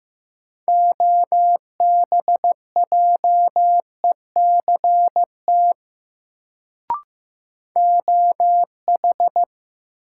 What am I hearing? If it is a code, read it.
Morse code, heard as a keyed tone, about 15 wpm, 700 Hz. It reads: OBJECT OH
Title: Sentences from Top 500 Words - New Word Drill - 15wpm